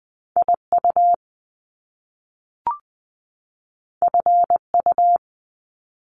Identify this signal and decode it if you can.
Morse code, heard as a keyed tone, about 20 wpm, 700 Hz. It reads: IU FU
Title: Top Two Letter Combinations in Words 20wpm